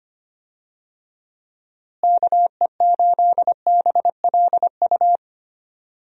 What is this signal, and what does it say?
Morse code, heard as a keyed tone, about 25 wpm, 700 Hz. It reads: KE8BLU
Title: USA Call Signs - 6 Characters - Repeated 3x - 25wpm